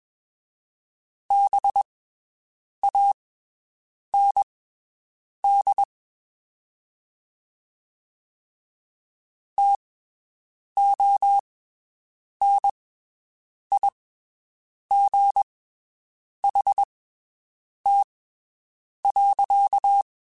Morse code, heard as a keyed tone, about 21 wpm, spaced as for 7 wpm, 800 Hz. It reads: BAND TONIGHT.
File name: (spaced) / QSOs Spaced-mp3s / WD0BGP 1 @20 sp